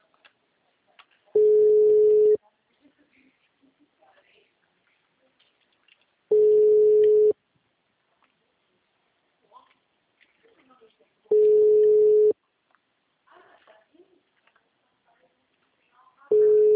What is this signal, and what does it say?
Ringback tone (Europe, 425 Hz), with speech.